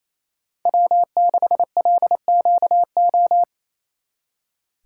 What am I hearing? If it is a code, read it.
Morse code, heard as a keyed tone, about 34 wpm, 700 Hz. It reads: W6LQO
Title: Speed Racing - USA Call Signs 28wpm